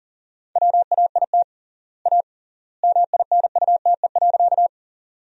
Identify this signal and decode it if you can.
Morse code, heard as a keyed tone, about 40 wpm, 700 Hz. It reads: WAIT A MINUTE.
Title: 3x Word-Spacing - Sentences from Top 500 Words - 40wpm